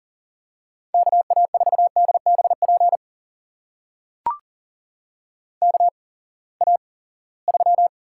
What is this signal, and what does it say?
Morse code, heard as a keyed tone, about 40 wpm, spaced as for 23 wpm, 700 Hz. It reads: KA4DBP KA3
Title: USA Call Signs - 6 Characters - 8x Character Spacing - 40wpm